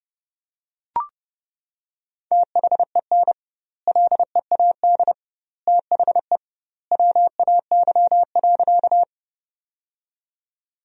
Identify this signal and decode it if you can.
Morse code, heard as a keyed tone, about 30 wpm, 700 Hz. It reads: THEN LEAD THE WAY.
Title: 2x Word-Spacing - Sentences from Top 500 Words - 30wpm